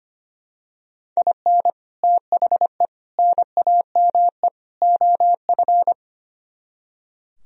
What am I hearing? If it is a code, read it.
Morse code, heard as a keyed tone, about 28 wpm, 700 Hz. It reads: IN THE NAME OF
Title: Speed-Racing - Sets of 4 Words 25wpm